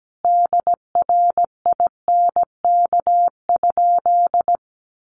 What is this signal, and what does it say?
Morse code, heard as a keyed tone, about 17 wpm, 700 Hz. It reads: DRINK?
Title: Sentences from Top 1500 Words 17wpm